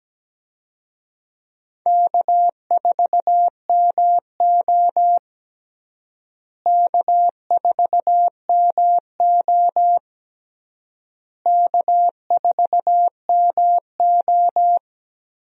Morse code, heard as a keyed tone, about 17 wpm, 700 Hz. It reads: K4MO K4MO K4MO
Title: USA Call Signs - 4 Characters - Repeated 3x - 17wpm